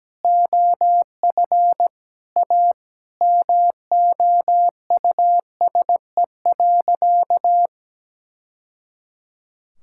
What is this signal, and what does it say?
Morse code, heard as a keyed tone, about 17 wpm, 700 Hz. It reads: OF A MOUSE.
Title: Sentences from Top 1000 Words - Review Part 5 - 17wpm